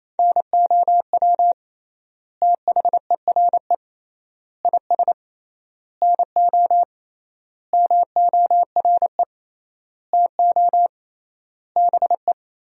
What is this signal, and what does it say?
Morse code, heard as a keyed tone, about 28 wpm, 700 Hz. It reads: NOW THERE IS NO MORE TO BE
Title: 3x Word-Spacing - Sentences from Top 100 Words - 28wpm